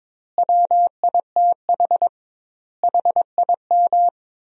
Morse code, heard as a keyed tone, about 22 wpm, 700 Hz. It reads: WITH HIM
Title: 2x Word-Spacing - Sentences from Top 500 Words - 22wpm